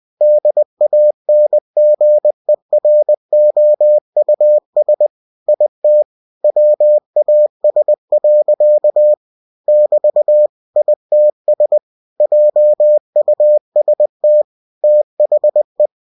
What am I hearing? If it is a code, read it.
Morse code, heard as a keyed tone, about 20 wpm, 600 Hz. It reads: DANGEROUS IT WAS. <BT> ITS JUST THE
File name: cw-books / winnie-the-pooh / winnie-20wpm-page-0082